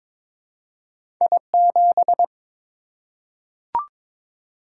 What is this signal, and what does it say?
Morse code, heard as a keyed tone, about 22 wpm, 690 Hz. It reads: I7
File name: practice / 22wpm / Two Letter-Number Combinations - 22wpm